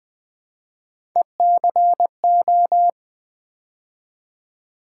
Morse code, heard as a keyed tone, about 20 wpm, 700 Hz. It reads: ECO
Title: Build A Word - 3 Letters At A Time - 8-Letter Words - 20wpm